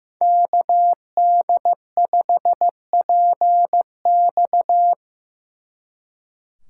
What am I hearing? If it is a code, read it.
Morse code, heard as a keyed tone, about 18 wpm, 700 Hz. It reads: KD5PX
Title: Speed Racing - USA 2x2 Call Signs 15wpm